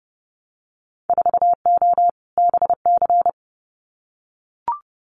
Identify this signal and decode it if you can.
Morse code, heard as a keyed tone, about 30 wpm, 700 Hz. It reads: 4O BC
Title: Winter Field Day Exchanges Encore 30wpm